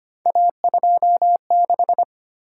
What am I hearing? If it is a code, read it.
Morse code, heard as a keyed tone, about 25 wpm, 700 Hz. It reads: A26
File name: Three Letter-Number Combinations - 25wpm